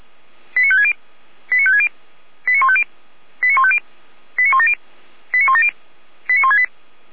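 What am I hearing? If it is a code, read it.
Five-tone selective call (ZVEI), heard as quick runs of five tones, tones of about 70 ms. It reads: 89560 89560 89150 89150 89170 89170 89168
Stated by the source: ZVEI five-tone selective call received by SDR, ID 89167, channel 455, from the audio archive of Feuerwehr Orlamünde.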